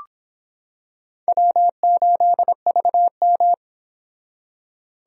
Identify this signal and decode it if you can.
Morse code, heard as a keyed tone, about 25 wpm, 700 Hz. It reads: W8VM